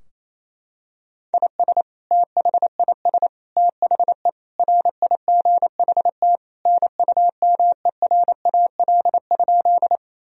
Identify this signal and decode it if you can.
Morse code, heard as a keyed tone, about 28 wpm, 700 Hz. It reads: IS THIS THE RIGHT NUMERAL?
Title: Sentences from Top 1000 Words - Review Part 2 - 28wpm